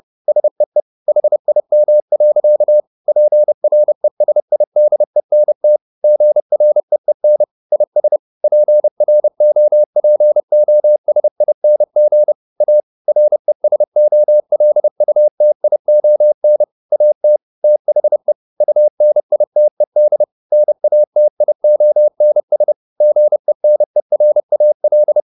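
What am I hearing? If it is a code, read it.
Morse code, heard as a keyed tone, about 30 wpm, 600 Hz. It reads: SEE HIM. PRESIDENT GREEN IS PROPOSING A RESOLUTION AT THE UNITED NATIONS GENERAL
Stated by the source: Morse code at 30 wpm, the novel Never by Ken Follett in English, from CW med Gnister.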